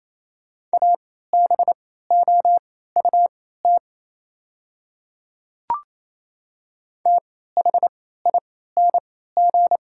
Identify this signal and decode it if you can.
Morse code, heard as a keyed tone, about 28 wpm, spaced as for 16 wpm, 700 Hz. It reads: ABOUT THING
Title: 3x Character Spacing - Top 100 Words - 28wpm